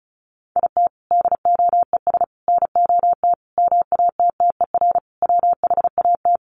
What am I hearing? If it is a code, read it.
Morse code, heard as a keyed tone, about 34 wpm, 700 Hz. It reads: IT DOES NOT MATTER WHAT
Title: Sets of 5 Words Encore 35wpm